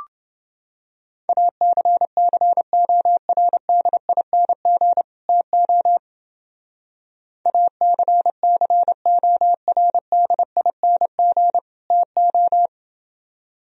Morse code, heard as a keyed tone, about 29 wpm, 700 Hz. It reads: ACCORDING TO ACCORDING TO